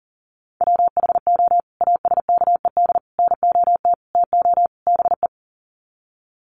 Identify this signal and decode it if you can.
Morse code, heard as a keyed tone, about 40 wpm, 700 Hz. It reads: WHO ASKED NOT TO BE
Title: Sets of 5 Words Encore2 40wpm